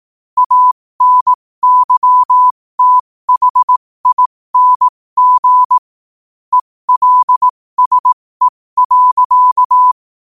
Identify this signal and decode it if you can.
Morse code, heard as a keyed tone, about 18 wpm, spaced as for 15 wpm, 1000 Hz. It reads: ANYTHING ELSE.
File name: qotdfiles / 15wpm